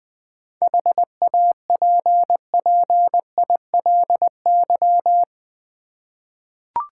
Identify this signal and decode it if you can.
Morse code, heard as a keyed tone, about 20 wpm, 700 Hz. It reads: HAPPILY
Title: 7 Letter Words - First 3 Letters Given - 20wpm